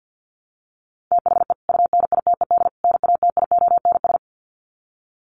Morse code, heard as a keyed tone, about 50 wpm, 700 Hz. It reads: THE UNITED NATIONS
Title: Sets of 3 Words 50wpm